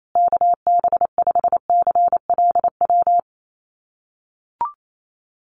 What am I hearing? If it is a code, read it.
Morse code, heard as a keyed tone, about 28 wpm, 700 Hz. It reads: KB5CLW
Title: USA Call Signs Repeated 2x 28wpm